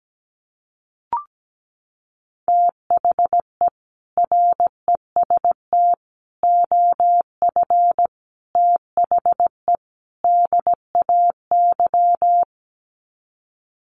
Morse code, heard as a keyed tone, about 17 wpm, 700 Hz. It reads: THE REST OF THE DAY
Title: Sets of 5 Words 17wpm